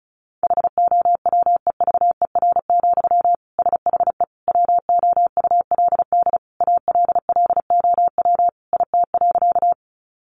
Morse code, heard as a keyed tone, about 34 wpm, 700 Hz. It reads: HOWEVER, SHE WOULD ALLOW IT.
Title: Sentences from Top 700 Words 35wpm